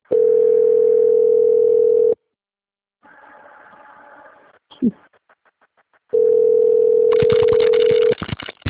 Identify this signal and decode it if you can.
Ringback tone (North America), 440 + 480 Hz.
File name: out- - - - - .28761